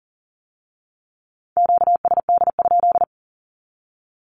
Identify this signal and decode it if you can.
Morse code, heard as a keyed tone, about 42 wpm, 700 Hz. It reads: QSD?
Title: Q-Codes with Spoken Meaning - 40wpm